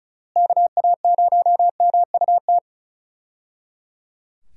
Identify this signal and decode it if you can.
Morse code, heard as a keyed tone, about 34 wpm, 700 Hz. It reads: KA0MUT
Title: USA Call Signs - 6 Characters - Repeated 3x - 35wpm